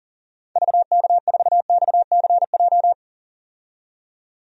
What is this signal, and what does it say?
Morse code, heard as a keyed tone, about 40 wpm, 700 Hz. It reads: VK4XCJ
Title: DXCC-100 Call Signs - 6 Characters - 40wpm